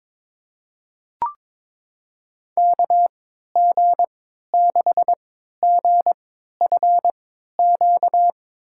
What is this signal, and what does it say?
Morse code, heard as a keyed tone, about 22 wpm, spaced as for 13 wpm, 700 Hz. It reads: KG6GFQ
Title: USA Call Signs - 6 Characters - 3x Character Spacing - 22wpm